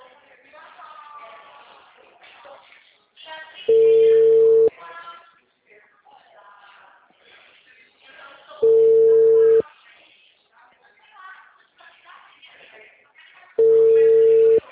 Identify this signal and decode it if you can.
Ringback tone (China, 450 Hz), with speech.